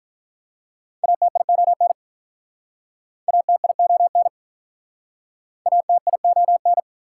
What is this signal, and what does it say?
Morse code, heard as a keyed tone, about 48 wpm, 700 Hz. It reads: ATION ATION ATION